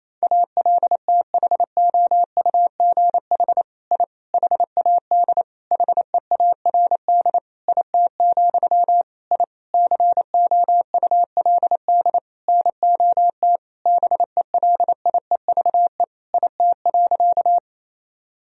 Morse code, heard as a keyed tone, about 28 wpm, 700 Hz. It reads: ALTHOUGH I HAD HEARD IT, I COULD NOT BELIEVE IT.
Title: Sentences from Top 1000 Words - Review Part 3 - 28wpm